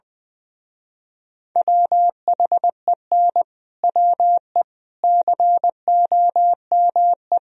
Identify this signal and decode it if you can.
Morse code, heard as a keyed tone, about 20 wpm, 700 Hz. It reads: WHEN WE COME
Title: Speed-Racing - Sets of 4 Words 20wpm